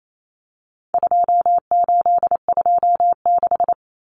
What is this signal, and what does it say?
Morse code, heard as a keyed tone, about 28 wpm, 700 Hz. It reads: 2826
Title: Numbers - 4-Digit 28wpm